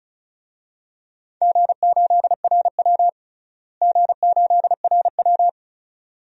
Morse code, heard as a keyed tone, about 36 wpm, 700 Hz. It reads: G8RW G8RW
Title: DXCC-100 Call Signs - 4 Characters - Repeated 2x - 35wpm